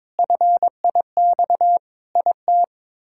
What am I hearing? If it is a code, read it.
Morse code, heard as a keyed tone, about 23 wpm, 700 Hz. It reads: FIX IT
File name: practice / 22wpm / Sets of 7 Words - 22wpm